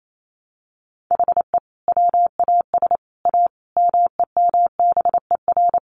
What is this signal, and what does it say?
Morse code, heard as a keyed tone, about 28 wpm, 700 Hz. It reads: HE WAS A MEMBER
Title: Sets of 5 Words Encore2 28wpm